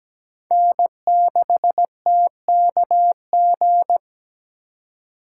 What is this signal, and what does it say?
Morse code, heard as a keyed tone, about 20 wpm, 700 Hz. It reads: N6TKG
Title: Speed Racing - USA 1x3 Call Signs 17wpm